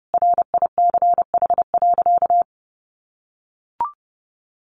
Morse code, heard as a keyed tone, about 30 wpm, 700 Hz. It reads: RICH.
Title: Sentences from Top 1100 Words 30wpm